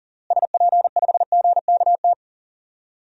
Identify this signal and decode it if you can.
Morse code, heard as a keyed tone, about 40 wpm, 700 Hz. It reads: SP5GKT
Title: DXCC-100 Call Signs - 6 Characters - Repeated 3x - 40wpm